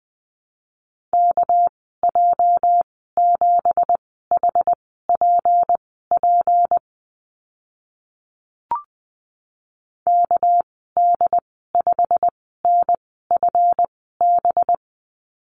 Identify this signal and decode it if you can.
Morse code, heard as a keyed tone, about 19 wpm, spaced as for 14 wpm, 700 Hz. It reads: KJ7HPP KD5NFB
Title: 207 Lesson 33g - Letter j - Call Signs 20wpm